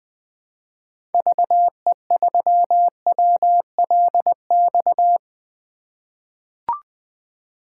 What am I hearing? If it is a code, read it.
Morse code, heard as a keyed tone, about 21 wpm, 700 Hz. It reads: VE3WLX
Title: 2023 POTA Activator Call Signs - 20wpm